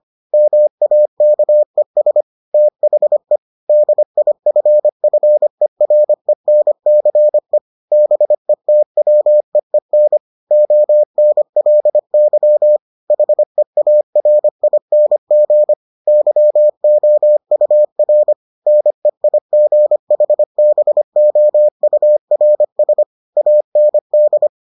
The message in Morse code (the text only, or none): MAKES THE DIFFERENCE BETWEEN ONLY HEARING YOUR NEIGHBOURS AND